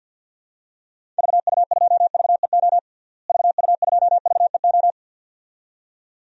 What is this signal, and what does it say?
Morse code, heard as a keyed tone, about 53 wpm, 700 Hz. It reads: VU2VEO VU2VEO
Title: DXCC-100 Call Signs - 6 Characters - Repeated 2x - 50wpm